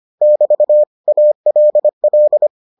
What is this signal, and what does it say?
Morse code, heard as a keyed tone, about 25 wpm, 600 Hz. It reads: <BT> ALL